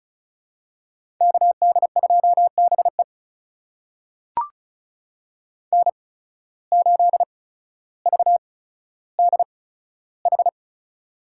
Morse code, heard as a keyed tone, about 36 wpm, spaced as for 15 wpm, 700 Hz. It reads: KD2BE N8VDH
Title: USA Call Signs - 5 Characters - 8x Character Spacing - 35wpm